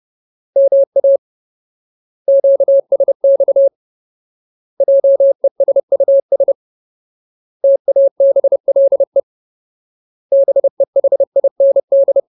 Morse code, heard as a keyed tone, about 30 wpm, 550 Hz. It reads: MA QSX JESUS TABLE BEHIND
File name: Ultimate Word List / 30WPM Ultimate